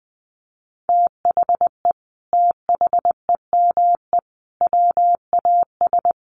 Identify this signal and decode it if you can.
Morse code, heard as a keyed tone, about 20 wpm, 700 Hz. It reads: THE THEME WAS
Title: Sentences from Top 1100 Words 20wpm